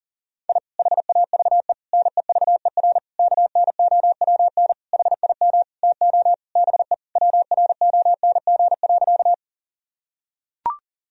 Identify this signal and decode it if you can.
Morse code, heard as a keyed tone, about 40 wpm, 700 Hz. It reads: I HAVE NEVER KNOWN HIM TO BE WRONG.
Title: Sentences from Top 1000 Words - Review Part 3 - 40wpm